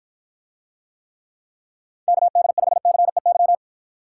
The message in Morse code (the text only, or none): KD5CEY